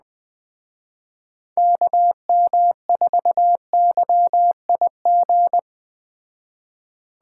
Morse code, heard as a keyed tone, about 19 wpm, 700 Hz. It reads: KM4YIG
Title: Speed Racing - USA 2x3 Call Signs 17wpm